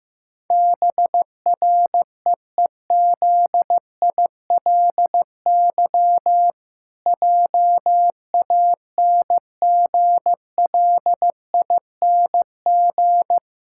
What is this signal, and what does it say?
Morse code, heard as a keyed tone, about 16 wpm, 700 Hz. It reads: BREEZILY JANGLING
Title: Pangram Challenge v2 - 15wpm